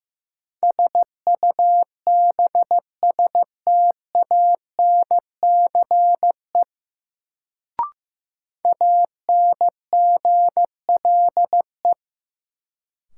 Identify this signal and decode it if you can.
Morse code, heard as a keyed tone, about 15 wpm, 700 Hz. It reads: SUBSTANCE ANGLE
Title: Sentences from Top 1500 Words - New Word Drill Alt - 15wpm